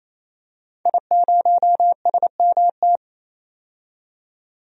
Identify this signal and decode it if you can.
Morse code, heard as a keyed tone, about 28 wpm, 700 Hz. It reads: I0SMT
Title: DXCC-100 Call Signs - 5 Characters - 28wpm